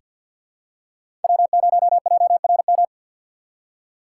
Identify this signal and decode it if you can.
Morse code, heard as a keyed tone, about 46 wpm, 700 Hz. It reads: W0JRM
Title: USA Call Signs - 5 Characters - 4x Character Spacing - 50wpm